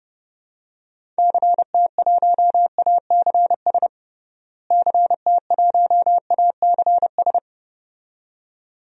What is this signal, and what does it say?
Morse code, heard as a keyed tone, about 31 wpm, 700 Hz. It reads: CT1ACS CT1ACS
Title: DXCC-100 Call Signs - 6 Characters - Repeated 2x - 30wpm